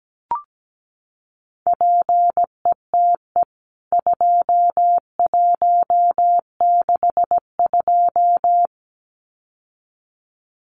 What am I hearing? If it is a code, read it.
Morse code, heard as a keyed tone, about 17 wpm, 700 Hz. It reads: PETE 2162
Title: CWT Exchange 17wpm